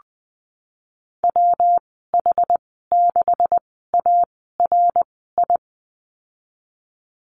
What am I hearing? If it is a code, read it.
Morse code, heard as a keyed tone, about 20 wpm, spaced as for 14 wpm, 700 Hz. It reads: WH6ARI
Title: 140 Lesson 23g - Letter f - Call Signs 20wpm